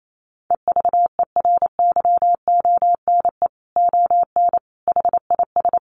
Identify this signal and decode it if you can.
Morse code, heard as a keyed tone, about 28 wpm, 700 Hz. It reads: EVERYONE ON HIS
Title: A Sound Of Thunder 28wpm